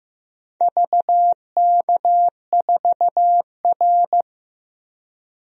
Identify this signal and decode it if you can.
Morse code, heard as a keyed tone, about 15 wpm, 700 Hz. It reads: VK4R
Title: DXCC-100 Call Signs - 4 Characters - 15wpm